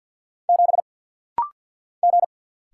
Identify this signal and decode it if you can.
Morse code, heard as a keyed tone, about 50 wpm, 700 Hz. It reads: 7 G
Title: Single Letter-Number - Rapid-Fire 50wpm